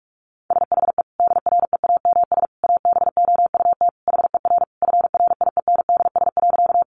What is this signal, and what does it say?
Morse code, heard as a keyed tone, about 46 wpm, 700 Hz. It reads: SHE DREAMS ABOUT HER FRIENDS.
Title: Easy Phrases 45wpm